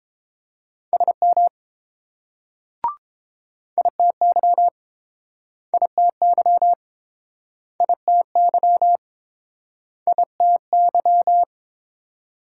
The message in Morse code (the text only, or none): SM ITY ITY ITY ITY